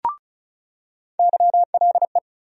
Morse code, heard as a keyed tone, about 36 wpm, 700 Hz. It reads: YLE
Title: USA Call Sign Postfixes - 3 Letters 35wpm